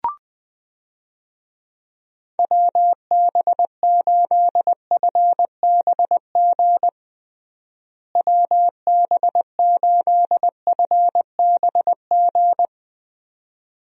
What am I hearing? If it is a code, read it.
Morse code, heard as a keyed tone, about 20 wpm, 700 Hz. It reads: WB8FBG WB8FBG
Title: USA Call Signs - 6 Characters - Repeated 2x - 20wpm